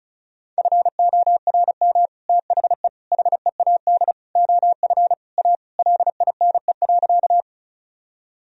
Morse code, heard as a keyed tone, about 34 wpm, 700 Hz. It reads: FORM THE HEAD OF A LINE.
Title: Sentences from Top 1000 Words - Review Part 1 - 35wpm